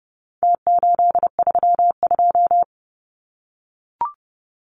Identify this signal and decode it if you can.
Morse code, heard as a keyed tone, about 30 wpm, 700 Hz. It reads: T832
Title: Numbers - Contest Style - 4-Digit 30wpm